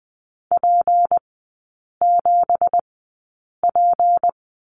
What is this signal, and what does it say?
Morse code, heard as a keyed tone, about 20 wpm, spaced as for 8 wpm, 700 Hz. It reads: P7P
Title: 166 Lesson 27e - Number 7 - Three Characters 20wpm